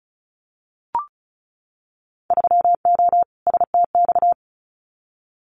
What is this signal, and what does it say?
Morse code, heard as a keyed tone, about 34 wpm, 700 Hz. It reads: 3O STX
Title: Winter Field Day Exchanges 35wpm